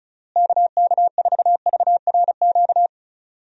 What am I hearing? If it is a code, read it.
Morse code, heard as a keyed tone, about 34 wpm, 700 Hz. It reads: KK4VRQ
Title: USA Call Signs - 6 Characters - Repeated 3x - 35wpm